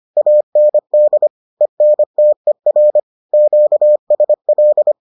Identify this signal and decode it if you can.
Morse code, heard as a keyed tone, about 25 wpm, 600 Hz. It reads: AND ENTER QSL